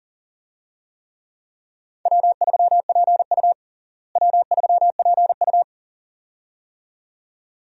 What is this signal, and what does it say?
Morse code, heard as a keyed tone, about 40 wpm, 700 Hz. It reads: W3PU W3PU
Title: USA Call Signs - 4 Characters - Repeated 2x - 40wpm